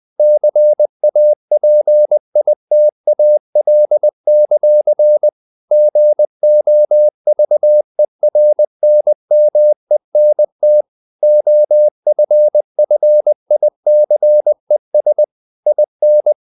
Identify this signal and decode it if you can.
Morse code, heard as a keyed tone, about 20 wpm, 600 Hz. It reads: CAPITAL; GOVERNMENT OFFICES IN
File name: CW / Practice / NationalCapitals20